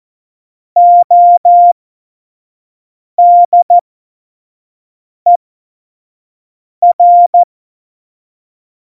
Morse code, heard as a keyed tone, about 14 wpm, spaced as for 5 wpm, 700 Hz. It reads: ODER
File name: Morsekurs / Witze / Witz 03.Dezember 2019, 25 F70